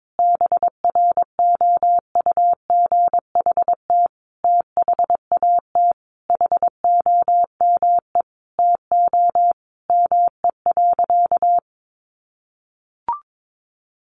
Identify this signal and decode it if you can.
Morse code, heard as a keyed tone, about 22 wpm, 700 Hz. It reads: BROUGHT THAT HOME TO ME.
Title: Sentences from Top 1100 Words 22wpm